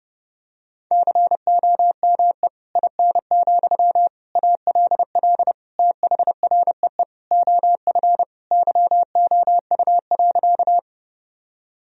Morse code, heard as a keyed tone, about 30 wpm, 700 Hz. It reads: COME IN, ALL THREE OF YOU.
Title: Sentences from Top 1000 Words - Review Part 1 - 30wpm